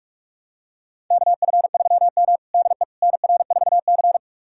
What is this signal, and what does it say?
Morse code, heard as a keyed tone, about 46 wpm, 700 Hz. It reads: KF3M DE NR4C
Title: USA Callsign DE Callsign - 4 Characters - 45wpm